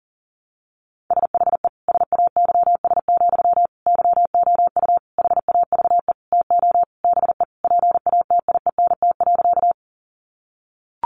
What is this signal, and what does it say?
Morse code, heard as a keyed tone, about 37 wpm, 700 Hz. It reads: SHE SAYS, YOU HAVE TO BE PATIENT.
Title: Easy Phrases 40wpm